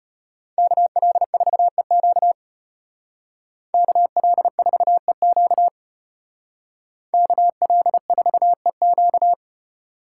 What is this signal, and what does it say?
Morse code, heard as a keyed tone, about 30 wpm, 700 Hz. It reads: KL4EQ KL4EQ KL4EQ
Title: Speed Racing - USA 2x2 Call Signs 25wpm